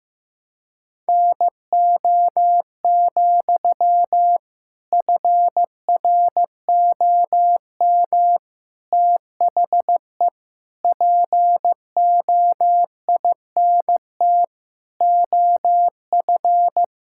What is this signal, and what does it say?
Morse code, heard as a keyed tone, about 16 wpm, 700 Hz. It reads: NO, FROM THE POINT OF
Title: Sentences from Top 1000 Words - Review Part 6 - 15wpm